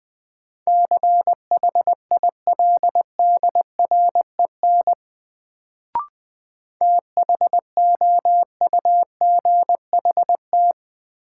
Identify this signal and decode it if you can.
Morse code, heard as a keyed tone, about 20 wpm, 700 Hz. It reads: CHILDREN THOUGHT
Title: Sentences from Top 300 Words - New Word Drill - 20wpm